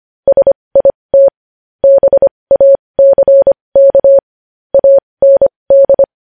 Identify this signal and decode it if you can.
Morse code, heard as a keyed tone, about 25 wpm, spaced as for 20 wpm, 550 Hz. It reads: SIT BACK AND
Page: News Headlines in Morse Code at 20 WPM